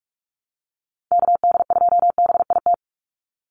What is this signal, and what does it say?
Morse code, heard as a keyed tone, about 44 wpm, 700 Hz. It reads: KD2BIT
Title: USA Call Signs 45wpm